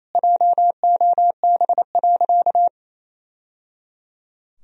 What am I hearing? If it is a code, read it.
Morse code, heard as a keyed tone, about 28 wpm, 700 Hz. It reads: JOB.
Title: Sentences from Top 1000 Words - Review Part 3 - 28wpm